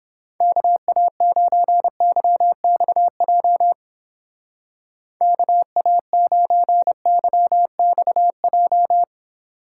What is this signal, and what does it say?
Morse code, heard as a keyed tone, about 31 wpm, 700 Hz. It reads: KA9YXJ KA9YXJ